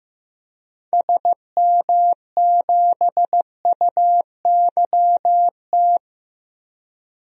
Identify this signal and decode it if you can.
Morse code, heard as a keyed tone, about 15 wpm, 700 Hz. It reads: SM7UYT
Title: DXCC-100 Call Signs - 6 Characters - Repeated 2x - 15wpm